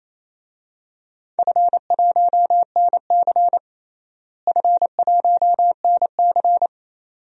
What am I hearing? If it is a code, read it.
Morse code, heard as a keyed tone, about 27 wpm, 700 Hz. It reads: F1NC F1NC
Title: DXCC-100 Call Signs - 4 Characters - Repeated 2x - 28wpm